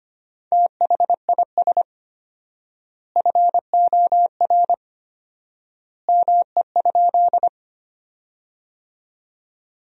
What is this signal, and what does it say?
Morse code, heard as a keyed tone, about 25 wpm, 700 Hz. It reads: THIS FOR ME?
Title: 4x Word-Spacing - Sentences from Top 200 Words - 25wpm